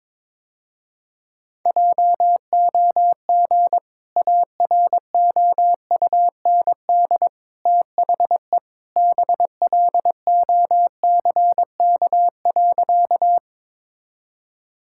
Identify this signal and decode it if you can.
Morse code, heard as a keyed tone, about 22 wpm, 700 Hz. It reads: JOG AROUND THE BLOCK.
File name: practice / 22wpm / Easy Sentences with Hints - 22wpm